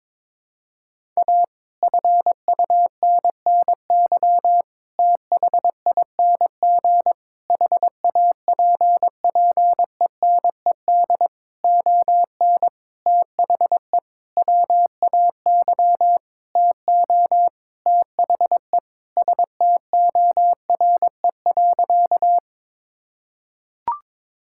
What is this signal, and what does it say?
Morse code, heard as a keyed tone, about 22 wpm, 700 Hz. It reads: A FUNNY THING HAPPENED ON THE WAY TO THE STORE.
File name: Sentences with 9-Word Phrases - 22wpm